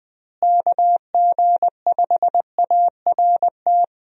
Morse code, heard as a keyed tone, about 20 wpm, 700 Hz. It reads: KG5ART
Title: USA Call Signs - 6 Characters - Repeated 2x - 20wpm